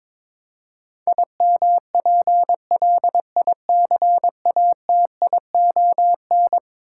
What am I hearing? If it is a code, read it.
Morse code, heard as a keyed tone, about 22 wpm, 700 Hz. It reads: IMPLICATION
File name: practice / 22wpm / Eleven Letter Words - 22wpm